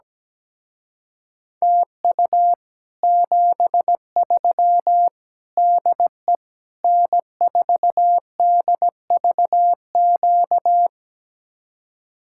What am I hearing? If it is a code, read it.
Morse code, heard as a keyed tone, about 17 wpm, 700 Hz. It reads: TU 73 DE N4DVQ
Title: Speed-Racing Common QSO Phrases Encore - 17wpm